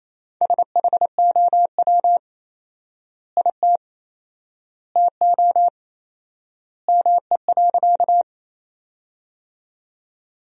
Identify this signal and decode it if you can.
Morse code, heard as a keyed tone, about 28 wpm, 700 Hz. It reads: SHOW IT TO ME.
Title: 4x Word-Spacing - Sentences from Top 200 Words - 28wpm